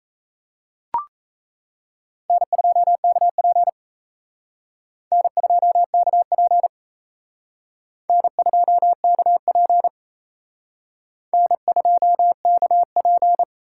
Morse code, heard as a keyed tone, about 35 wpm, 700 Hz. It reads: N2KP N2KP N2KP N2KP
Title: Speed Racing - USA 1x2 Call Signs 28wpm